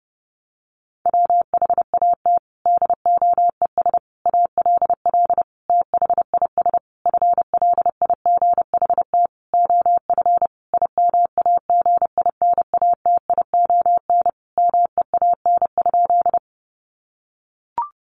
Morse code, heard as a keyed tone, about 30 wpm, 700 Hz. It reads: WHAT DOES ALL THIS FLIGHT OF IMAGINATION MEAN?
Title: Sentences from Top 1300 Words 30wpm